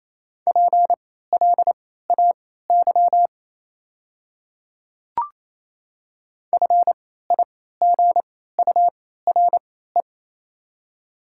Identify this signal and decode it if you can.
Morse code, heard as a keyed tone, about 28 wpm, spaced as for 16 wpm, 700 Hz. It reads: PLAY FIGURE
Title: 3x Character Spacing - Top 500 Words - 28wpm